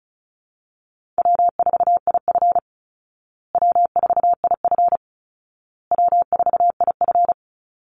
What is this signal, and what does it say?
Morse code, heard as a keyed tone, about 34 wpm, 700 Hz. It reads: W4IF W4IF W4IF
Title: Easy Call Signs Repeated 3x - 35wpm